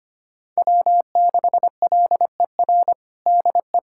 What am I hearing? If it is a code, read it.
Morse code, heard as a keyed tone, about 25 wpm, 700 Hz. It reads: W6LER DE
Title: USA Callsign DE Callsign 25wpm